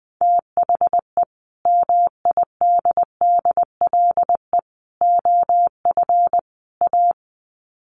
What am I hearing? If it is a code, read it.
Morse code, heard as a keyed tone, about 20 wpm, 700 Hz. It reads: THE MIDDLE OF A
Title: Sets of 4 Words 20wpm